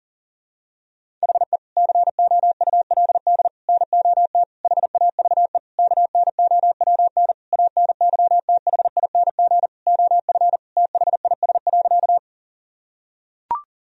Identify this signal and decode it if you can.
Morse code, heard as a keyed tone, about 40 wpm, 700 Hz. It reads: HE COULD NOT HAVE KNOWN ANYTHING OF THIS.
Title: Sentences from Top 1000 Words - Review Part 3 - 40wpm